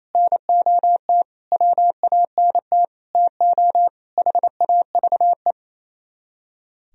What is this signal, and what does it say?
Morse code, heard as a keyed tone, about 36 wpm, 700 Hz. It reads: NOT WANT TO HAVE